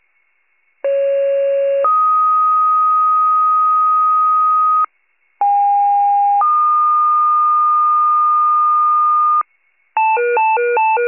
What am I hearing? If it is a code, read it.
Two-tone page, heard as a one-second tone, then a three-second tone, about 550 then 1250 Hz, about 810 then 1220 Hz. It is heard from 0.8 to 9.4 s.
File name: wcalert